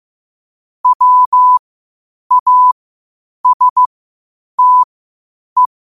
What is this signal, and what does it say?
Morse code, heard as a keyed tone, about 15 wpm, spaced as for 8 wpm, 1000 Hz. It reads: WASTE
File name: qotdfiles / 8wpm